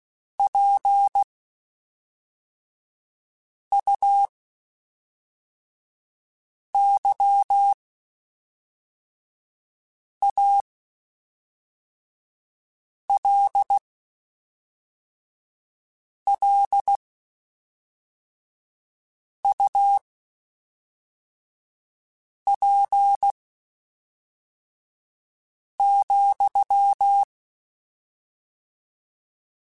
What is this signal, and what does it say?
Morse code, heard as a keyed tone, about 16 wpm, spaced as for 3 wpm, 800 Hz. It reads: PUYALLUP,